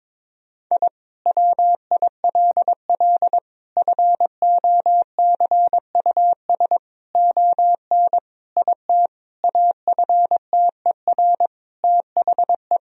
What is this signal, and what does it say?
Morse code, heard as a keyed tone, about 22 wpm, 700 Hz. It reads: I WILL FOCUS ON IT AFTER THE
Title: Sentences from Top 1000 Words - Review Part 4 - 22wpm